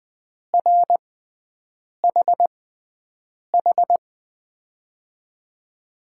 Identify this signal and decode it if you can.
Morse code, heard as a keyed tone, about 21 wpm, spaced as for 7 wpm, 700 Hz. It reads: RHH
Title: USA Call Signs - 6 Characters - 6x Character Spacing - 20wpm